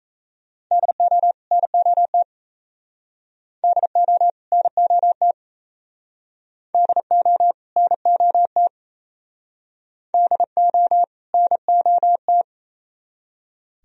Morse code, heard as a keyed tone, about 34 wpm, 700 Hz. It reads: DO NOT DO NOT DO NOT DO NOT
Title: Speed-Racing - Sets of 2 Words 28wpm